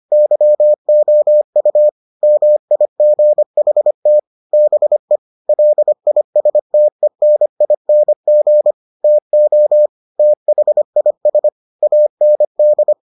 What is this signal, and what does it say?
Morse code, heard as a keyed tone, about 25 wpm, 600 Hz. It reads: YOU MIGHT BE LISTENING TO THIS AND